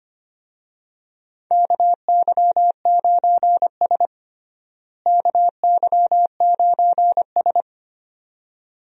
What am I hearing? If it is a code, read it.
Morse code, heard as a keyed tone, about 25 wpm, 700 Hz. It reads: KY9S KY9S
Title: USA Call Signs - 4 Characters - Repeated 2x - 25wpm